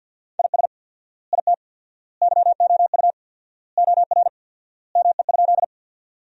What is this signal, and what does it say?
Morse code, heard as a keyed tone, about 47 wpm, 700 Hz. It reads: IS IT YOU OR ME?
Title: 4x Word-Spacing - Sentences from Top 200 Words - 50wpm